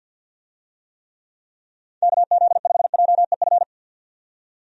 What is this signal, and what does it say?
Morse code, heard as a keyed tone, about 51 wpm, 700 Hz. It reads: KZ5JEF
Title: USA Call Signs - 6 Characters - 50wpm